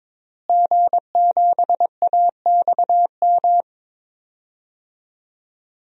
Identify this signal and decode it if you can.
Morse code, heard as a keyed tone, about 22 wpm, 700 Hz. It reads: G7AXM
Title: DXCC-100 Call Signs - 5 Characters - Repeated 2x - 22wpm